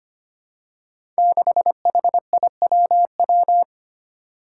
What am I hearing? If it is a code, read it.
Morse code, heard as a keyed tone, about 25 wpm, 700 Hz. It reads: 6HIWW
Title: Five Challenging Letter-Number Combinations 25wpm